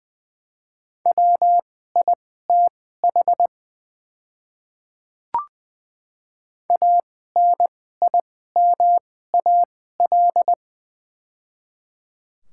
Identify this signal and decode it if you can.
Morse code, heard as a keyed tone, about 20 wpm, spaced as for 14 wpm, 700 Hz. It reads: WITH ANIMAL
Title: 2x Character Spacing - Top 200 Words - 20wpm